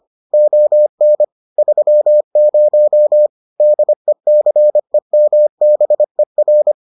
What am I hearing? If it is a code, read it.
Morse code, heard as a keyed tone, about 25 wpm, 600 Hz. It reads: ON 30 DECEMBER